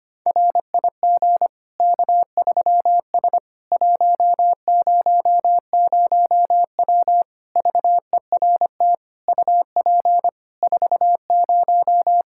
Morse code, heard as a keyed tone, about 25 wpm, 700 Hz. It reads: RIG K3S 100W VERT UP 40
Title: K1USN SST QSO Practice 25wpm